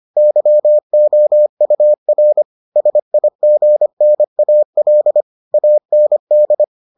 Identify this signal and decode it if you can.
Morse code, heard as a keyed tone, about 25 wpm, 600 Hz. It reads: YOUR SIGNAL AND